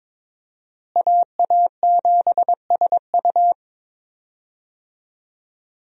Morse code, heard as a keyed tone, about 21 wpm, 700 Hz. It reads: AA7SU
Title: USA Call Signs - 5 Characters - Repeated 3x - 22wpm